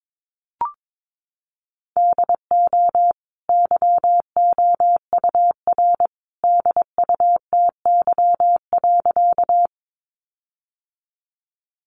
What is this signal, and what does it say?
Morse code, heard as a keyed tone, about 22 wpm, 700 Hz. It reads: DO YOUR DUTY.
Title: Sentences from Top 1500 Words 22wpm